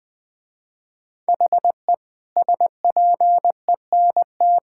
Morse code, heard as a keyed tone, about 20 wpm, 700 Hz. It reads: HE SPENT
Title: Sentences with 5-Word Phrases - Spoken Beginning - 20wpm